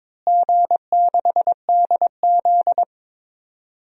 Morse code, heard as a keyed tone, about 22 wpm, 700 Hz. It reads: G6DZ
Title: DXCC-100 Call Signs - 4 Characters - Repeated 3x - 22wpm